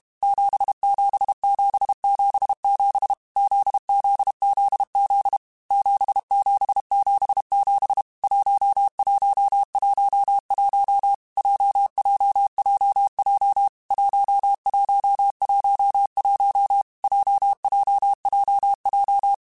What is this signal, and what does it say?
Morse code, heard as a keyed tone, about 32 wpm, 800 Hz. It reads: 77777 ZZZZ 7777 1111 JJJJ 1111 JJJJ
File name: Basic Letter Groups By7Groups / JZ17 / JZ17@30